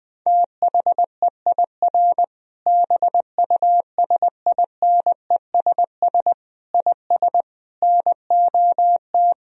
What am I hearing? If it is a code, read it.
Morse code, heard as a keyed tone, about 20 wpm, 700 Hz. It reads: THEIR BUSINESS IS NOT
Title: Sentences from Top 1000 Words - Review Part 3 - 20wpm